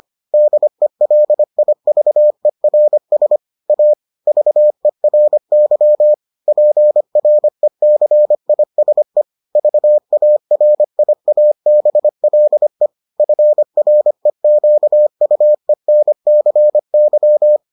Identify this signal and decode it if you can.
Morse code, heard as a keyed tone, about 25 wpm, 600 Hz. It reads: DELIVERS A VERY PRECISE VARIABLE FREQUENCY